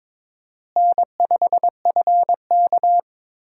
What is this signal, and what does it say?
Morse code, heard as a keyed tone, about 22 wpm, 700 Hz. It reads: N5FK
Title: USA Call Signs - 4 Characters - Repeated 2x - 22wpm